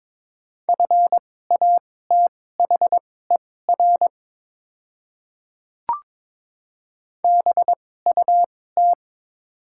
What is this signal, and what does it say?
Morse code, heard as a keyed tone, about 22 wpm, spaced as for 16 wpm, 700 Hz. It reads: FATHER BUT
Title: 2x Character Spacing - Top 200 Words - 22wpm